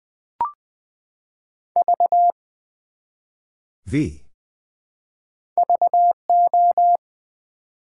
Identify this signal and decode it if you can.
Morse code, heard as a keyed tone, about 20 wpm, 700 Hz. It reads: V VO
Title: Build A Word - 1 Letter At A Time - 3 to 5-Letter Words - 20wpm